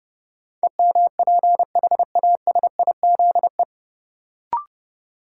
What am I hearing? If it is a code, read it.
Morse code, heard as a keyed tone, about 32 wpm, 700 Hz. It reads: EMPHASIZE
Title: Sentences from Top 1500 Words - New Word Drill Alt - 30wpm